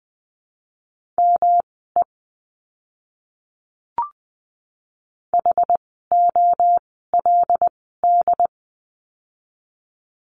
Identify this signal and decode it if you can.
Morse code, heard as a keyed tone, about 20 wpm, spaced as for 15 wpm, 700 Hz. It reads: ME HOLD
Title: 115 Lesson 19f - Letter w - Words 20wpm